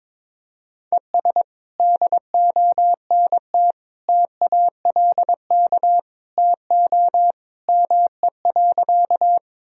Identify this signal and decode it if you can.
Morse code, heard as a keyed tone, about 22 wpm, 700 Hz. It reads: ES DONT TALK TO ME.
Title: Sentences from Top 400 Words and Abbreviations 22wpm